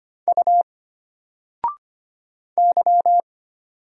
Morse code, heard as a keyed tone, about 25 wpm, 700 Hz. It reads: U Y
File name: Single-Letter-Number-25wpm